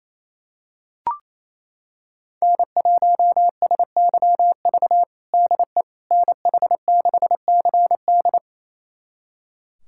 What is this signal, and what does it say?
Morse code, heard as a keyed tone, about 27 wpm, 700 Hz. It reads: N1SYV DE NH6CD
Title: USA Callsign DE Callsign - 5 Characters - 28wpm